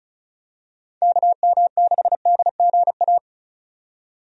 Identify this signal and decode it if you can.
Morse code, heard as a keyed tone, about 36 wpm, 700 Hz. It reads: KM6DGA
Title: USA Call Signs - 6 Characters - 35wpm